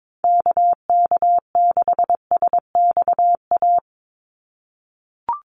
Morse code, heard as a keyed tone, about 22 wpm, 700 Hz. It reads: KK6SXA
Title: USA Call Signs 22wpm